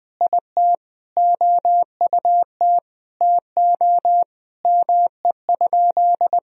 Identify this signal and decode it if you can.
Morse code, heard as a keyed tone, about 20 wpm, 700 Hz. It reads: IT OUT TO ME?
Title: Sentences from Top 200 Words and Abbreviations 20wpm